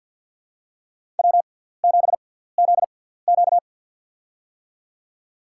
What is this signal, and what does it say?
Morse code, heard as a keyed tone, about 53 wpm, spaced as for 17 wpm, 700 Hz. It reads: W7ZQ
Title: USA Call Signs - 6x Character Spacing - 50wpm